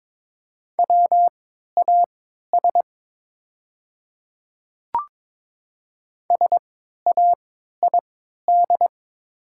Morse code, heard as a keyed tone, about 23 wpm, spaced as for 13 wpm, 700 Hz. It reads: WAS SAID
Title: 3x Character Spacing - Top 500 Words - 22wpm